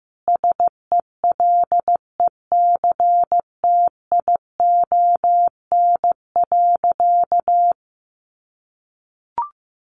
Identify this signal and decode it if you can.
Morse code, heard as a keyed tone, about 15 wpm, 700 Hz. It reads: SELECTION.
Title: Sentences from Top 1400 Words 15wpm